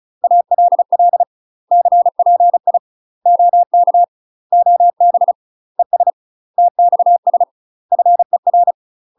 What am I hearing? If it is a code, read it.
Morse code, heard as a keyed tone, about 36 wpm, 700 Hz. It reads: ALL CPI OK OB ES TXS FER
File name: UK MASTER QSOs TRAINING/35WPM